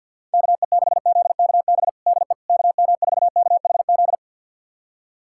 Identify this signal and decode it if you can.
Morse code, heard as a keyed tone, about 52 wpm, 700 Hz. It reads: KE6ZKB DE KM4KHZ
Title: USA Callsign DE Callsign 50wpm